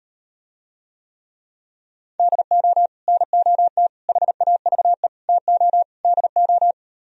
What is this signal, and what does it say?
Morse code, heard as a keyed tone, about 38 wpm, 700 Hz. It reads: DO NOT HAVE TO DO